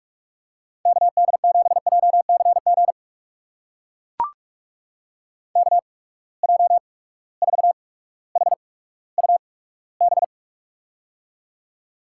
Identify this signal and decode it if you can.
Morse code, heard as a keyed tone, about 43 wpm, spaced as for 19 wpm, 700 Hz. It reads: KD7JCG KJ4HUB
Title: USA Call Signs - 6 Characters - 8x Character Spacing - 45wpm